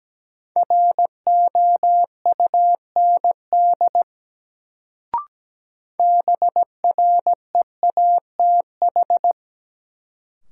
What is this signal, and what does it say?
Morse code, heard as a keyed tone, about 17 wpm, 700 Hz. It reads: ROUND BREATH
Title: Sets of 5 Words Encore - New Word Drill - 17wpm